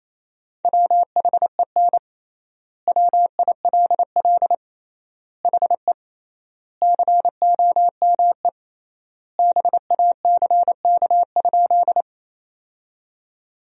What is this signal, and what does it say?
Morse code, heard as a keyed tone, about 28 wpm, 700 Hz. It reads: WHEN WILL HE COME BACK?
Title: 3x Word-Spacing - Sentences from Top 200 Words - 28wpm